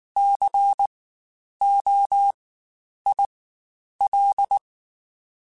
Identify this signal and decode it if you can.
Morse code, heard as a keyed tone, about 19 wpm, spaced as for 9 wpm, 800 Hz. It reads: COIL
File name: Spaced Alphas / Spaced Alphas mp3s / Cs sp @18